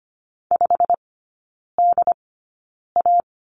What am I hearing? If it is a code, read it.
Morse code, heard as a keyed tone, about 25 wpm, spaced as for 9 wpm, 700 Hz. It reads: 5DA
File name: Lesson 15e - Number 5 - Three Characters 25wpm